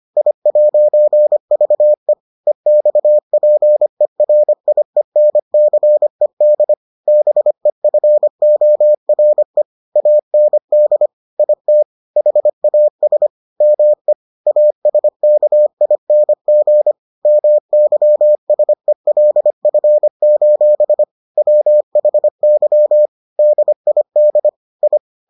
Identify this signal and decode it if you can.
Morse code, heard as a keyed tone, about 25 wpm, 600 Hz. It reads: I'VE EXPERIENCED BEFORE AND IT HAS ME ASKING MYSELF: WHY DID I